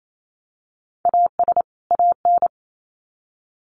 Morse code, heard as a keyed tone, about 28 wpm, 700 Hz. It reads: AS AN
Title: Sets of 2 Words 28wpm